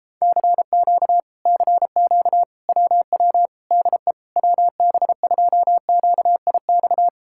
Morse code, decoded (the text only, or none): CQ CQ WW DE WB2QIX